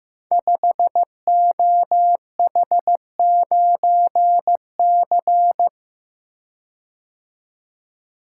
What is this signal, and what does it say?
Morse code, heard as a keyed tone, about 15 wpm, 700 Hz. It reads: 5OH9C
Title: Code Groups - 5 Letter:Numbers - 15wpm